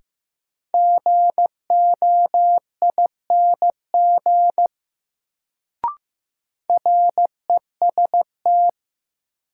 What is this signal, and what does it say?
Morse code, heard as a keyed tone, about 15 wpm, 700 Hz. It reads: GOING REST
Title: Sentences from Top 500 Words - New Word Drill - 15wpm